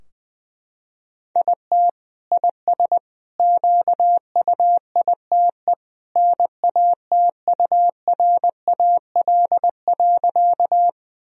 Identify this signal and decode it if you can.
Morse code, heard as a keyed tone, about 20 wpm, 700 Hz. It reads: IT IS QUITE NATURAL.
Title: Sentences from Top 1000 Words - Review Part 4 - 20wpm